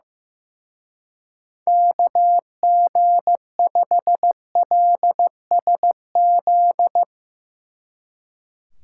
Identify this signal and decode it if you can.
Morse code, heard as a keyed tone, about 16 wpm, 700 Hz. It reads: KG5LSZ
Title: Speed Racing - USA 2x3 Call Signs 15wpm